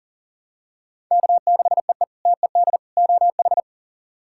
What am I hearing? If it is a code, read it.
Morse code, heard as a keyed tone, about 40 wpm, 700 Hz. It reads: K6EE TED OH
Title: K1USN SST QSO Practice - 40wpm